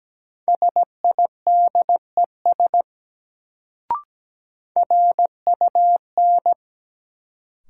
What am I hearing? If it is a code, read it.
Morse code, heard as a keyed tone, about 17 wpm, 700 Hz. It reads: SIDES RUN
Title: Sets of 4 Words - New Word Drill - 17wpm